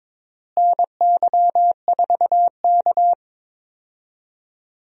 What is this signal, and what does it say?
Morse code, heard as a keyed tone, about 22 wpm, 700 Hz. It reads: NY4K
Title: USA Call Signs - 4 Characters - Repeated 3x - 22wpm